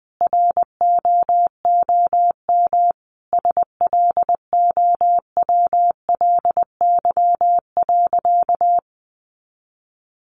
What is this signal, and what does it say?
Morse code, heard as a keyed tone, about 20 wpm, 700 Hz. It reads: ROOM SLOWLY.
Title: Sentences from Top 1100 Words 20wpm